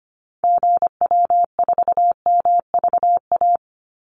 Morse code, heard as a keyed tone, about 25 wpm, 700 Hz. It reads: GW4MVA
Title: Call Signs - 25wpm